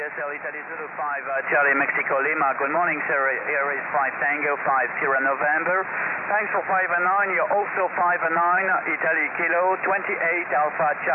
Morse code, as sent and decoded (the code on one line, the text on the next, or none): none